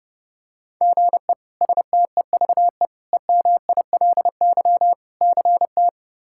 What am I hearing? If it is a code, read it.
Morse code, heard as a keyed tone, about 30 wpm, 700 Hz. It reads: GE STEVE EMILY CT
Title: K1USN SST QSO Practice - 30wpm